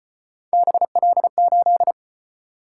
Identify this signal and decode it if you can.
Morse code, heard as a keyed tone, about 33 wpm, 700 Hz. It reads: BL8